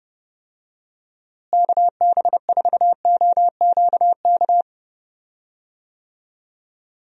Morse code, heard as a keyed tone, about 30 wpm, 700 Hz. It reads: KB4OQK
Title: USA Call Signs - 6 Characters - Repeated 3x - 30wpm